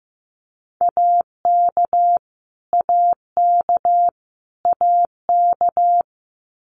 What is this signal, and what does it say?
Morse code, heard as a keyed tone, about 15 wpm, 700 Hz. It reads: AK AK AK
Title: US States Repeated 3x 15wpm